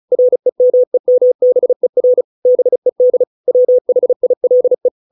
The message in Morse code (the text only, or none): REMEMBER BED WHILE